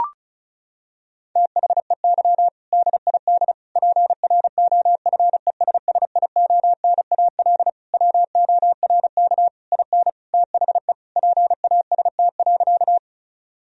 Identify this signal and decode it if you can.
Morse code, heard as a keyed tone, about 36 wpm, 700 Hz. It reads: THEY DID PROFESSIONAL WORK IN THE PAST.
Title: Sentences from Top 1000 Words - Review Part 4 - 35wpm